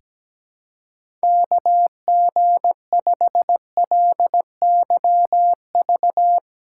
Morse code, heard as a keyed tone, about 17 wpm, 700 Hz. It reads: KG5LYV
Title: USA Callsign DE Callsign 17wpm